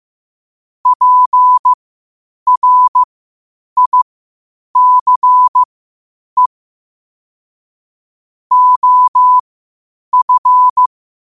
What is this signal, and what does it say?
Morse code, heard as a keyed tone, about 15 wpm, spaced as for 8 wpm, 1000 Hz. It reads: PRICE OF
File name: qotdfiles / 8wpm